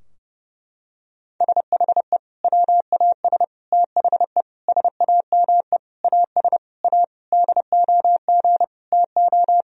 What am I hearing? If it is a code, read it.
Morse code, heard as a keyed tone, about 30 wpm, 700 Hz. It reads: SHE WAS THE SAME AS A DOG TO
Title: Sentences from Top 1000 Words - Review Part 2 - 30wpm